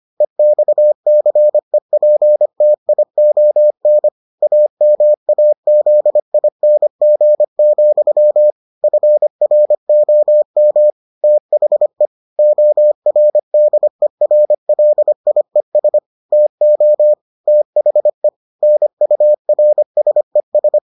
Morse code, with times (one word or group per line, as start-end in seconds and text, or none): 0.2-4.1 EXCEPTION
4.4-8.5 AMAZING,
8.8-10.9 FROM
11.2-12.1 THE
12.4-16.0 ORDERLIES
16.3-17.1 TO
17.5-18.3 THE
18.6-20.8 NURSES